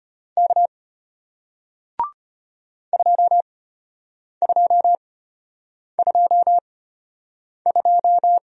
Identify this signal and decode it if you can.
Morse code, heard as a keyed tone, about 33 wpm, spaced as for 8 wpm, 700 Hz. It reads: K 2222